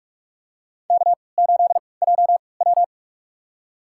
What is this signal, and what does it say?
Morse code, heard as a keyed tone, about 44 wpm, spaced as for 25 wpm, 690 Hz. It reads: K8JW
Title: USA Call Signs - 4 Characters - 3x Character Spacing - 45wpm